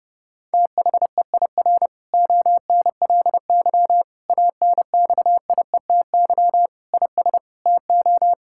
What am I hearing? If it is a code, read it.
Morse code, heard as a keyed tone, about 30 wpm, 700 Hz. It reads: THEIR ONLY ANXIETY IS TO
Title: Sentences from Top 1600 Words 30wpm